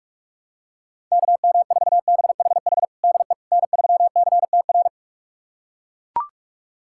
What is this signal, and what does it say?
Morse code, heard as a keyed tone, about 41 wpm, 700 Hz. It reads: KM4BHH DE N3CTR
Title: USA Callsign DE Callsign 45wpm